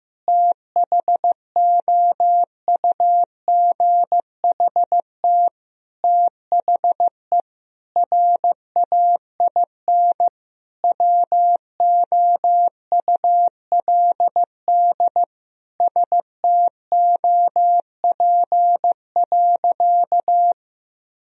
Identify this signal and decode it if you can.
Morse code, heard as a keyed tone, about 16 wpm, 700 Hz. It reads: THOUGHT THE RAIN WOULD STOP.
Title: Sentences from Top 1000 Words - Review Part 1 - 15wpm